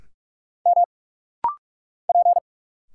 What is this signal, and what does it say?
Morse code, heard as a keyed tone, about 46 wpm, 690 Hz. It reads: M P
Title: Single Letters - Rapid-Fire 45wpm